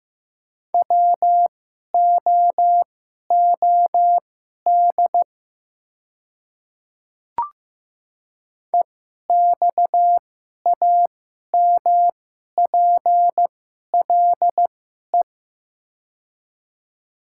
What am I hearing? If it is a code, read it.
Morse code, heard as a keyed tone, about 15 wpm, spaced as for 11 wpm, 690 Hz. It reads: WOOD EXAMPLE
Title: 2x Character Spacing - Top 300 Words - 15wpm